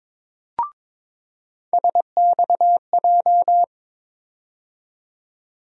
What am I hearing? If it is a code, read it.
Morse code, heard as a keyed tone, about 22 wpm, 700 Hz. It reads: SXJ
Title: USA Call Sign Postfixes - 3 Letters 22wpm